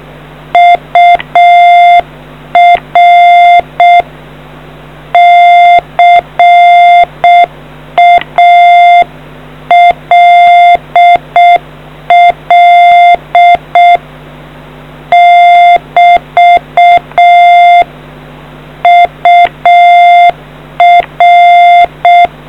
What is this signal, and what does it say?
Morse code, heard as a keyed tone, about 6 wpm, 700 Hz. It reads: UR CALL <BT> UR